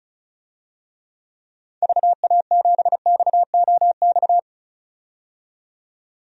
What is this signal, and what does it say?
Morse code, heard as a keyed tone, about 36 wpm, 690 Hz. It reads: VA7XOX
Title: DXCC-100 Call Signs - 6 Characters - 35wpm